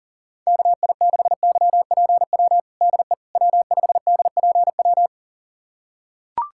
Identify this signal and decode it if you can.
Morse code, heard as a keyed tone, about 40 wpm, 700 Hz. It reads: KI6YPW DE W5DPW
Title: USA Callsign DE Callsign 40wpm